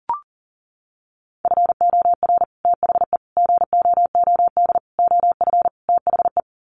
Morse code, heard as a keyed tone, about 37 wpm, 700 Hz. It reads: FOR THE GOOD OF THE
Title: Sets of 5 Words Encore2 40wpm